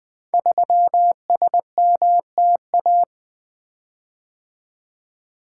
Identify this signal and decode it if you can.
Morse code, heard as a keyed tone, about 20 wpm, 700 Hz. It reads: 3SMTA
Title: Code Groups - 5 Letter:Numbers - 20wpm